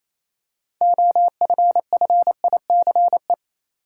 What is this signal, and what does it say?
Morse code, heard as a keyed tone, about 28 wpm, 700 Hz. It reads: OFFICE
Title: Six Letter Words - 28wpm